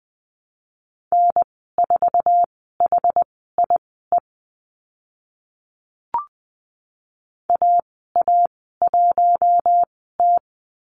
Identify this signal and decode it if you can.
Morse code, heard as a keyed tone, about 20 wpm, spaced as for 14 wpm, 700 Hz. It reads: N4HIE AA1T
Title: 062 Lesson 11g - Letter h - Call Signs 20wpm